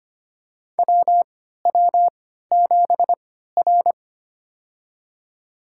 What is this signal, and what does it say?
Morse code, heard as a keyed tone, about 25 wpm, spaced as for 14 wpm, 700 Hz. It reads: WW7R
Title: USA Call Signs - 4 Characters - 3x Character Spacing - 25wpm